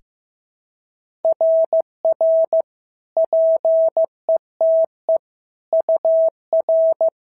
Morse code, heard as a keyed tone, about 15 wpm, 650 Hz. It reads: RR PETE UR
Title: SOTA QSO Practice 15wpm